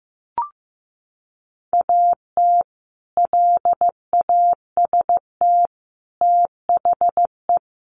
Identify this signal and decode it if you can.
Morse code, heard as a keyed tone, about 15 wpm, 700 Hz. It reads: AT LAST THE
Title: Sentences from Top 1100 Words 15wpm